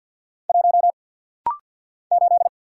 Morse code, heard as a keyed tone, about 50 wpm, 700 Hz. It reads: 1 8
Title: Single Letter-Number - Rapid-Fire 50wpm